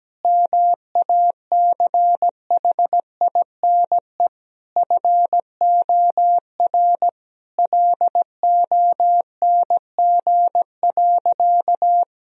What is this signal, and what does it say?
Morse code, heard as a keyed tone, about 17 wpm, 700 Hz. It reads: MACHINE FOR LONG.
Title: Sentences from Top 1000 Words - Review Part 2 - 17wpm